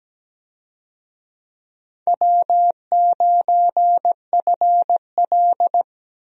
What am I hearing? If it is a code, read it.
Morse code, heard as a keyed tone, about 17 wpm, 700 Hz. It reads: W9FL
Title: USA Call Signs - 4 Characters - Repeated 2x - 17wpm